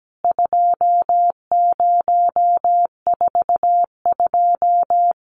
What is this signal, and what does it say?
Morse code, heard as a keyed tone, about 17 wpm, 700 Hz. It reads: 2042
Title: CWT Exchange 17wpm